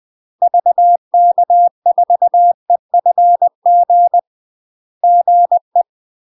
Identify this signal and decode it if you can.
Morse code, heard as a keyed tone, about 20 wpm, 700 Hz. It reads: VK4EFG GE
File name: UK MASTER QSOs TRAINING / 20WPM / UK MASTER QSOs 0029